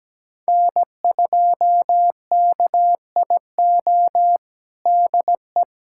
Morse code, heard as a keyed tone, about 17 wpm, 700 Hz. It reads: N2KIO DE
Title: USA Callsign DE Callsign - 5 Characters - 17wpm